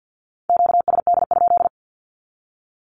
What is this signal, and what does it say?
Morse code, heard as a keyed tone, about 50 wpm, 700 Hz. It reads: QSD?
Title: Q-Codes with Spoken Meaning - 50wpm